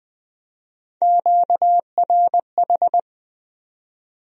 Morse code, heard as a keyed tone, about 20 wpm, 700 Hz. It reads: QRH
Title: Speed-Racing Q-Codes - 17wpm